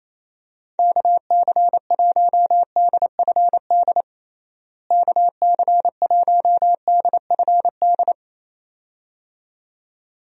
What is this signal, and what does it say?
Morse code, heard as a keyed tone, about 28 wpm, 700 Hz. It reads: KC1DFD KC1DFD
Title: USA Call Signs - 6 Characters - Repeated 2x - 28wpm